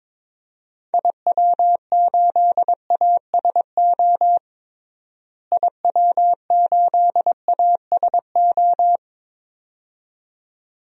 Morse code, heard as a keyed tone, about 22 wpm, 700 Hz. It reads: IW8ASO IW8ASO
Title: DXCC-100 Call Signs - 6 Characters - Repeated 2x - 22wpm